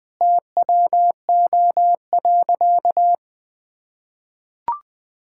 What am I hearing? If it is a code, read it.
Morse code, heard as a keyed tone, about 20 wpm, 700 Hz. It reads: TWO.
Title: Sentences from Top 1000 Words - Review Part 5 - 20wpm